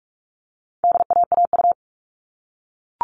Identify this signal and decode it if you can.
Morse code, heard as a keyed tone, about 44 wpm, 700 Hz. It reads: DAAU
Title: All Cut Numbers - 4-Digit 45wpm